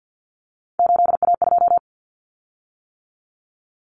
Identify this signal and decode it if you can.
Morse code, heard as a keyed tone, about 50 wpm, 700 Hz. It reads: 8A2
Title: Common Cut Numbers - 3-Digit 50wpm